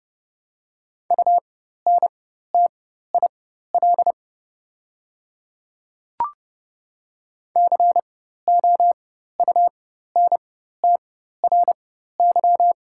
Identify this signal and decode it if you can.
Morse code, heard as a keyed tone, about 30 wpm, spaced as for 14 wpm, 690 Hz. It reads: UNTIL COUNTRY
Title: 4x Character Spacing - Top 300 Words - 30wpm